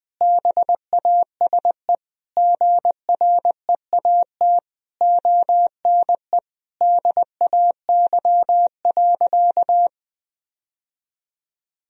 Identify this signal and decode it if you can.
Morse code, heard as a keyed tone, about 20 wpm, 700 Hz. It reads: BASE GREAT ONE DAY.
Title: Sentences from Top 1000 Words - Review Part 1 - 20wpm